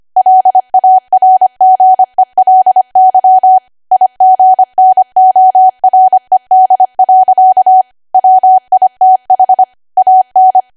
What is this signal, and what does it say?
Morse code, heard as a keyed tone, about 25 wpm, 750 Hz. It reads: LARGELY IGNORED. WITH AN
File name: Archive / 25 WPM / 25WPM